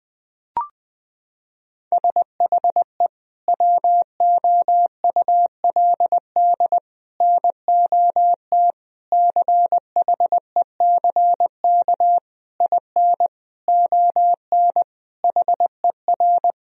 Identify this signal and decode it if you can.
Morse code, heard as a keyed tone, about 20 wpm, 700 Hz. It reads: SHE WOULD NOT CHECK IN ON HER
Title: Sentences from Top 1000 Words - Review Part 3 - 20wpm